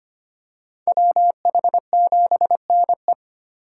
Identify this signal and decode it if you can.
Morse code, heard as a keyed tone, about 25 wpm, 700 Hz. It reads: WH7NE
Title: USA Call Signs - 4x Character Spacing - 25wpm